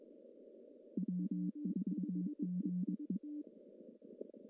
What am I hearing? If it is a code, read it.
Morse code, heard as a keyed tone, about 22 wpm, 190 Hz. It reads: W4GE
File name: CW / cqww-cw-k5tr